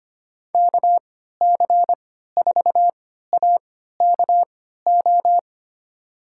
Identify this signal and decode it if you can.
Morse code, heard as a keyed tone, about 25 wpm, spaced as for 14 wpm, 700 Hz. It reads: KC4AKO
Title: USA Call Signs - 6 Characters - 3x Character Spacing - 25wpm